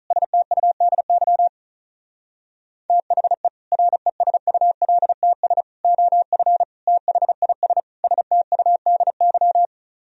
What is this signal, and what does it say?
Morse code, heard as a keyed tone, about 42 wpm, 700 Hz. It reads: STUDY THE RESULTS OF THIS STUDY
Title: Speed-Racing - Sets of 5 Words 35wpm